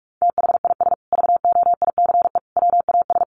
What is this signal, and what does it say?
Morse code, heard as a keyed tone, about 46 wpm, 700 Hz. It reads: THIS VOICE WAS
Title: Sentences from Top 500 Words 45wpm